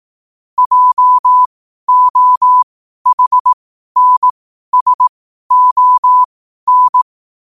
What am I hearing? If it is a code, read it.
Morse code, heard as a keyed tone, about 18 wpm, spaced as for 12 wpm, 1000 Hz. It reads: JOHNSON
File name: qotdfiles / 12wpm